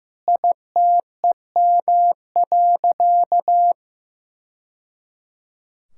Morse code, heard as a keyed tone, about 16 wpm, 700 Hz. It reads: ITEM.
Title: Sentences from Top 1000 Words - Review Part 6 - 15wpm